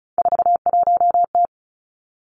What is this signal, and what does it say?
Morse code, heard as a keyed tone, about 36 wpm, 700 Hz. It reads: 41T
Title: Three Challenging Letter-Number Combinations 35wpm